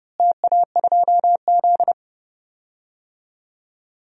Morse code, heard as a keyed tone, about 30 wpm, 700 Hz. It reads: TA2Z
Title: DXCC-100 Call Signs - 4 Characters - Repeated 3x - 30wpm